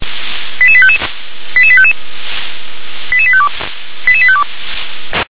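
Five-tone selective call (ZVEI), heard as quick runs of five tones, tones of about 70 ms. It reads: 88855 88855 88852 88852
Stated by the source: Received on an SDR; ZVEI five-tone call.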